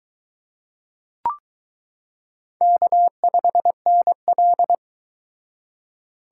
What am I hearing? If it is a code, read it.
Morse code, heard as a keyed tone, about 22 wpm, 700 Hz. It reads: K5NL